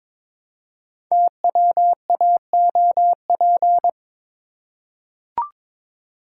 Morse code, heard as a keyed tone, about 22 wpm, 700 Hz. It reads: TWAOP
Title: Code Groups - 5 Letters - 22wpm